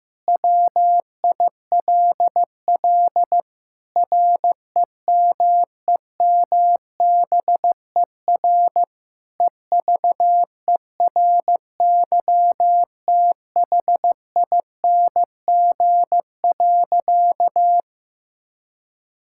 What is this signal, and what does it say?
Morse code, heard as a keyed tone, about 16 wpm, 700 Hz. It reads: WILL REMEMBER EVERYTHING.
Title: Sentences from Top 1000 Words - Review Part 5 - 15wpm